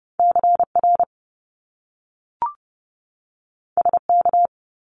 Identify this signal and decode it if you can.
Morse code, heard as a keyed tone, about 30 wpm, 700 Hz. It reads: CR SK
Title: Top Two Letter Combinations in Words 30wpm